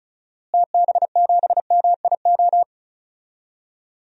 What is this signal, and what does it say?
Morse code, heard as a keyed tone, about 36 wpm, 700 Hz. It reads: TB7MIO
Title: DXCC-100 Call Signs - 6 Characters - Repeated 2x - 35wpm